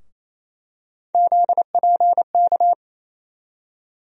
Morse code, heard as a keyed tone, about 28 wpm, 700 Hz. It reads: ZPK
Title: USA Call Sign Postfixes - 3 Letters 28wpm